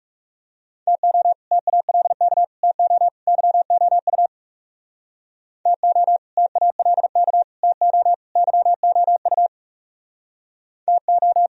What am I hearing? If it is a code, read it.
Morse code, heard as a keyed tone, about 36 wpm, 700 Hz. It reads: TO TALK TO YOU TO TALK TO YOU TO
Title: Speed-Racing - Sets of 4 Words 30wpm